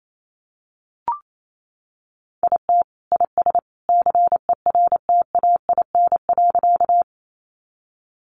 Morse code, heard as a keyed tone, about 28 wpm, 700 Hz. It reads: IT IS CERTAIN.
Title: Sentences from Top 500 Words 28wpm